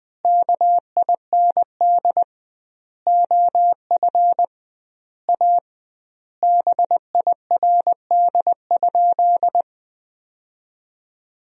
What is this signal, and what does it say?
Morse code, heard as a keyed tone, about 21 wpm, 700 Hz. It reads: KIND OF A BIRD?
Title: 2x Word-Spacing - Sentences from Top 400 Words - 20wpm